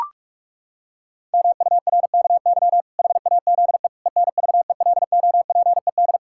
Morse code, heard as a keyed tone, about 41 wpm, 700 Hz. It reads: MURKY HAZE ENVELOPED
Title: Pangram Challenge v2 - 45wpm